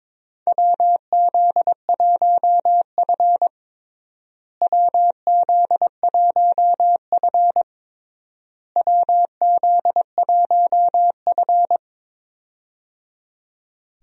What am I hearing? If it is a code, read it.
Morse code, heard as a keyed tone, about 22 wpm, 700 Hz. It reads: WZ1F WZ1F WZ1F
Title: USA Call Signs - 4 Characters - Repeated 3x - 22wpm